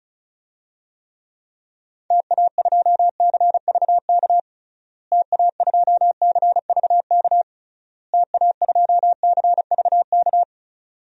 Morse code, heard as a keyed tone, about 34 wpm, 700 Hz. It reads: TA2CVK TA2CVK TA2CVK
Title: DXCC-100 Call Signs - 6 Characters - Repeated 3x - 35wpm